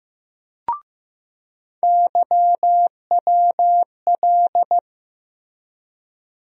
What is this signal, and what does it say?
Morse code, heard as a keyed tone, about 15 wpm, 700 Hz. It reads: YWL
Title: USA Call Sign Postfixes - 3 Letters 15wpm